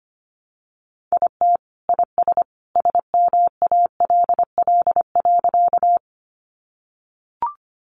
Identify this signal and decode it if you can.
Morse code, heard as a keyed tone, about 25 wpm, 700 Hz. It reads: IT IS SMALL.
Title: Additional Easy Phrases 25wpm